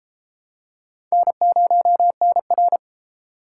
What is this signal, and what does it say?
Morse code, heard as a keyed tone, about 31 wpm, 700 Hz. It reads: N0NR